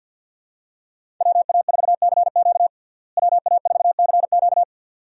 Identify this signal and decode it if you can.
Morse code, heard as a keyed tone, about 51 wpm, 700 Hz. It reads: WA4CQ WA4CQ
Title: USA Call Signs - 5 Characters - Repeated 2x - 50wpm